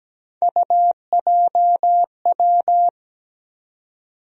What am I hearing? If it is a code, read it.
Morse code, heard as a keyed tone, about 17 wpm, 700 Hz. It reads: UJW
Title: USA Call Sign Postfixes - 3 Letters 17wpm